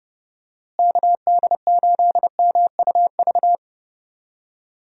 Morse code, heard as a keyed tone, about 32 wpm, 700 Hz. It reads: KD8MUV
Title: USA Call Signs - 6 Characters - Repeated 3x - 30wpm